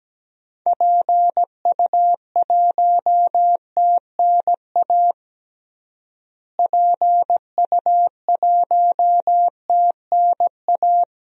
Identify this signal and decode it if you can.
Morse code, heard as a keyed tone, about 17 wpm, 700 Hz. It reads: PU1TNA PU1TNA
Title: DXCC-100 Call Signs - 6 Characters - Repeated 2x - 17wpm